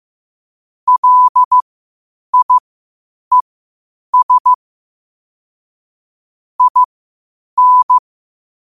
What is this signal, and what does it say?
Morse code, heard as a keyed tone, about 15 wpm, spaced as for 8 wpm, 1000 Hz. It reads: LIES IN
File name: qotdfiles / 8wpm